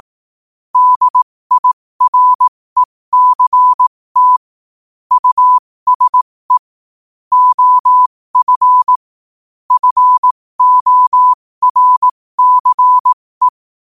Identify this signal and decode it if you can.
Morse code, heard as a keyed tone, about 18 wpm, spaced as for 15 wpm, 1000 Hz. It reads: DIRECT USE OF FORCE
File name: qotdfiles / 15wpm